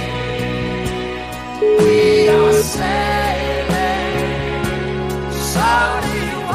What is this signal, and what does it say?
Ringback tone (Europe, 425 Hz), with noise.